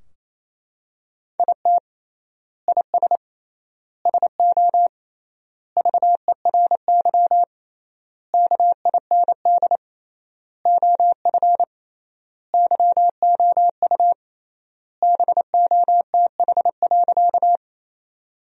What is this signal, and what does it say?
Morse code, heard as a keyed tone, about 27 wpm, 700 Hz. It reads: IT IS SO VERY KIND OF YOU BOTH.
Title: 3x Word-Spacing - Sentences from Top 300 Words - 28wpm